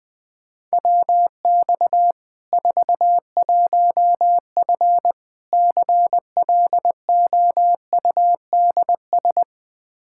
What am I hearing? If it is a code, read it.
Morse code, heard as a keyed tone, about 20 wpm, 700 Hz. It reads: WX 41F CLOUDS
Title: Practice QSO - Exchange 2 - Info - 20wpm